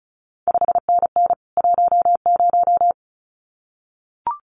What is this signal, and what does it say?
Morse code, heard as a keyed tone, about 36 wpm, 700 Hz. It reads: 5NN 10
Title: CQ WW DX Contest Exchange Encore - 35wpm